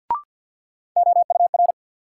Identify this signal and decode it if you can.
Morse code, heard as a keyed tone, about 51 wpm, 700 Hz. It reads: OUR
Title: Sentences from Top 200 Words - New Word Drill - 50wpm